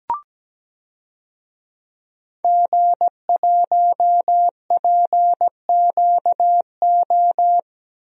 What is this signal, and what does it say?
Morse code, heard as a keyed tone, about 17 wpm, 700 Hz. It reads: G1PQO
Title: DXCC-100 Call Signs - 5 Characters - Repeated 3x - 17wpm